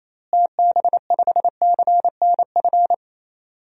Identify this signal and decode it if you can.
Morse code, heard as a keyed tone, about 28 wpm, 700 Hz. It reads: TB5CNF
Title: DXCC-100 Call Signs - 6 Characters - Repeated 2x - 28wpm